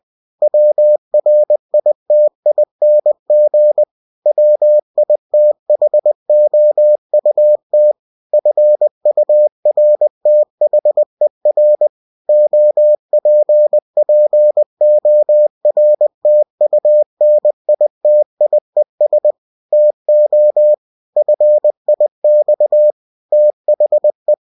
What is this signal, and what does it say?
Morse code, heard as a keyed tone, about 20 wpm, 600 Hz. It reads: WRITING WITHOUT FURTHER OPPORTUNITIES TO FIX THE
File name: cw-books / winnie-the-pooh / winnie-20wpm-page-0119